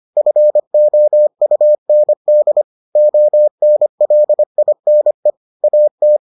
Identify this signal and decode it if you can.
Morse code, heard as a keyed tone, about 25 wpm, 600 Hz. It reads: FOUND ONLINE AT